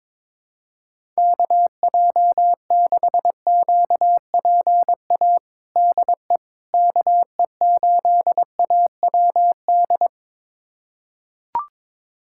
Morse code, heard as a keyed tone, about 22 wpm, 700 Hz. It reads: KJ6QPA DE KE8AWD
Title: USA Callsign DE Callsign - 6 Characters - 22wpm